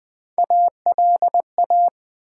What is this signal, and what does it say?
Morse code, heard as a keyed tone, about 20 wpm, 700 Hz. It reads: ALA
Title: Build A Word - 1 Letter At A Time - 5-Letter Words - 20wpm